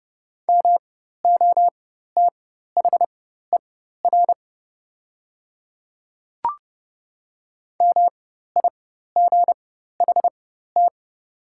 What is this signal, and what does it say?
Morse code, heard as a keyed tone, about 30 wpm, spaced as for 14 wpm, 700 Hz. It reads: MOTHER MIGHT
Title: 4x Character Spacing - Top 300 Words - 30wpm